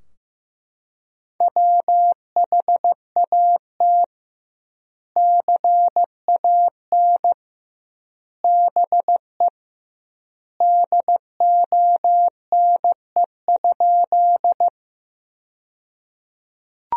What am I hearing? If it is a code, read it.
Morse code, heard as a keyed tone, about 15 wpm, 700 Hz. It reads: WHAT CAN BE DONE?
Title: 2x Word-Spacing - Sentences from Top 500 Words - 15wpm